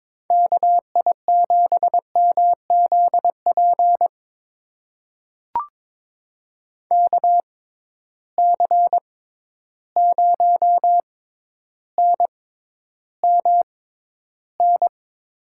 Morse code, heard as a keyed tone, about 22 wpm, spaced as for 11 wpm, 700 Hz. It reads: KI7MZP KC0NMN
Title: USA Call Signs - 6 Characters - 6x Character Spacing - 22wpm